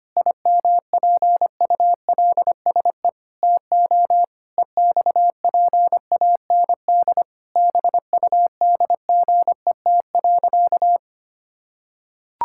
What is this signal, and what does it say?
Morse code, heard as a keyed tone, about 25 wpm, 700 Hz. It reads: IMPULSE TO EXPAND BUDGET.
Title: Pangram Challenge v2 - 25wpm